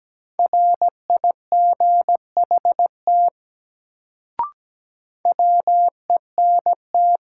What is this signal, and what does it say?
Morse code, heard as a keyed tone, about 17 wpm, 700 Hz. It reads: RIGHT WENT
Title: Sentences from Top 200 Words - New Word Drill - 17wpm